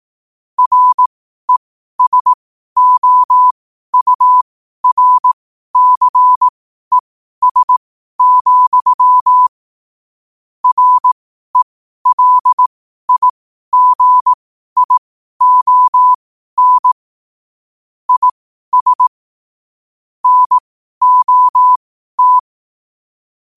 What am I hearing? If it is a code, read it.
Morse code, heard as a keyed tone, about 18 wpm, spaced as for 12 wpm, 1000 Hz. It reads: RESOURCES, RELIGION IS NOT
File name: qotdfiles / 12wpm